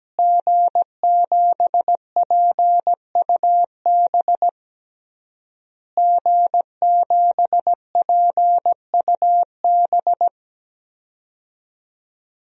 Morse code, heard as a keyed tone, about 17 wpm, 700 Hz. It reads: G7PUB G7PUB
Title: DXCC-100 Call Signs - 5 Characters - Repeated 3x - 17wpm